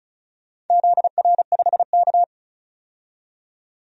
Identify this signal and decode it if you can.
Morse code, heard as a keyed tone, about 36 wpm, 700 Hz. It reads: ZR5K
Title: DXCC-100 Call Signs - 4 Characters - 35wpm